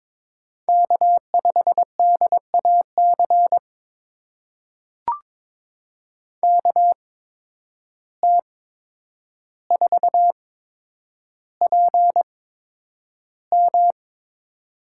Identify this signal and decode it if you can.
Morse code, heard as a keyed tone, about 22 wpm, spaced as for 9 wpm, 700 Hz. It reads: K5DAC KT4PM
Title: USA Call Signs - 5 Characters - 8x Character Spacing - 22wpm